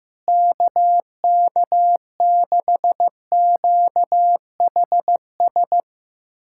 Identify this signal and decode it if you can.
Morse code, heard as a keyed tone, about 15 wpm, 700 Hz. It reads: KK6QHS
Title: USA Callsign DE Callsign 15wpm